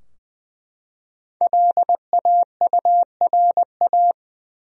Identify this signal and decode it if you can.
Morse code, heard as a keyed tone, about 20 wpm, 700 Hz. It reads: LAURA
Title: USA Common Names - Max 5 Letters - 20wpm